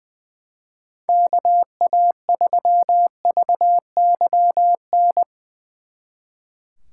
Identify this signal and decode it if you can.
Morse code, heard as a keyed tone, about 23 wpm, 700 Hz. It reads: KA3VYN
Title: Speed Racing - USA Call Signs 20wpm